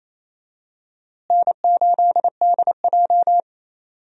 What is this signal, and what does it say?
Morse code, heard as a keyed tone, about 27 wpm, 700 Hz. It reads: N8DJ
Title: USA Call Signs - 4 Characters - Repeated 2x - 28wpm